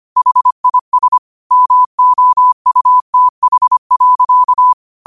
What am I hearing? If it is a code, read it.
Morse code, heard as a keyed tone, about 25 wpm, 1000 Hz. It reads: HIS MOUTH.
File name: qotdfiles / 25wpm